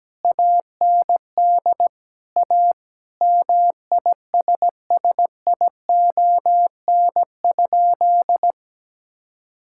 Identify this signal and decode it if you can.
Morse code, heard as a keyed tone, about 17 wpm, 700 Hz. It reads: AND A MISSION?
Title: Sentences from Top 1600 Words 17wpm